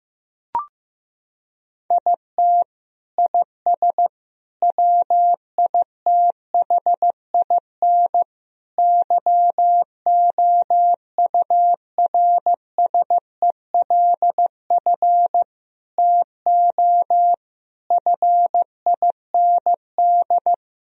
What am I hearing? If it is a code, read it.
Morse code, heard as a keyed tone, about 15 wpm, 700 Hz. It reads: IT IS WITHIN YOURSELF TO FIND
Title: Sentences from Top 1000 Words - Review Part 4 - 15wpm